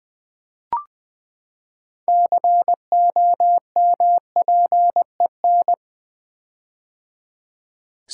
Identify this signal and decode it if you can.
Morse code, heard as a keyed tone, about 20 wpm, 700 Hz. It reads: COMPEN
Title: Build A Word - 6 Letters At A Time - 12-Letter Words - 20wpm